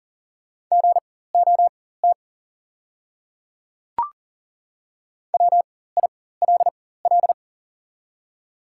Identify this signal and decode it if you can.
Morse code, heard as a keyed tone, about 40 wpm, spaced as for 19 wpm, 690 Hz. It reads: GOT WILL
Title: 4x Character Spacing - Top 300 Words - 40wpm